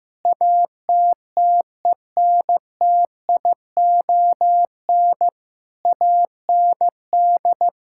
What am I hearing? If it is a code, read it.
Morse code, heard as a keyed tone, about 15 wpm, 700 Hz. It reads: ATTENTION AND
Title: Sentences from Top 1000 Words - Review Part 6 - 15wpm